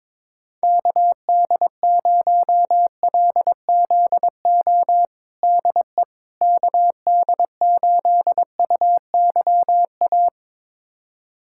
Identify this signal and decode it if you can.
Morse code, heard as a keyed tone, about 22 wpm, 700 Hz. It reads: KD0LZO DE KD8UYA
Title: USA Callsign DE Callsign - 6 Characters - 22wpm